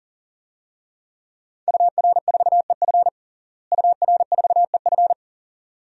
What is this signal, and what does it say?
Morse code, heard as a keyed tone, about 40 wpm, 700 Hz. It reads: UR4EF UR4EF
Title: DXCC-100 Call Signs - 5 Characters - Repeated 2x - 40wpm